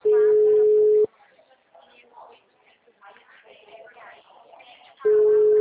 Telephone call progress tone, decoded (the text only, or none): ringback tone